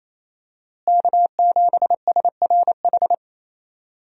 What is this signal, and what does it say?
Morse code, heard as a keyed tone, about 27 wpm, 700 Hz. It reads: K7SRH
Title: USA Call Signs - 5 Characters - Repeated 3x - 28wpm